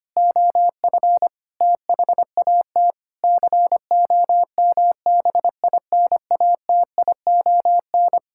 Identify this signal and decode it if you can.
Morse code, heard as a keyed tone, about 25 wpm, 700 Hz. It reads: OF THAT COMBINATION
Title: Sentences from Top 1000 Words - Review Part 6 - 25wpm